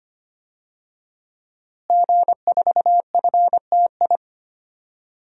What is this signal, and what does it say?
Morse code, heard as a keyed tone, about 25 wpm, 700 Hz. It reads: G4FTI
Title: DXCC-100 Call Signs - 5 Characters - Repeated 3x - 25wpm